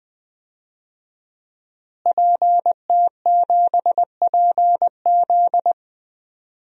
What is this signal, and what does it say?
Morse code, heard as a keyed tone, about 21 wpm, 700 Hz. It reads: PT7PZ
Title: DXCC-100 Call Signs - 5 Characters - Repeated 2x - 20wpm